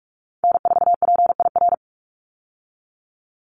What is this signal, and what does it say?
Morse code, heard as a keyed tone, about 44 wpm, 700 Hz. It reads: N4PIR
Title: Call Signs Repeated 2x - 45wpm